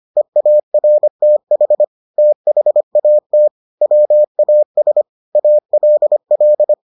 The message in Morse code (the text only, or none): EARTH THAT WAS ALL